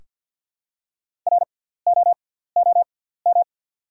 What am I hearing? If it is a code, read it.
Morse code, heard as a keyed tone, about 48 wpm, spaced as for 17 wpm, 690 Hz. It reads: ROOM
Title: 6x Character Spacing - Top 500 Words - 50wpm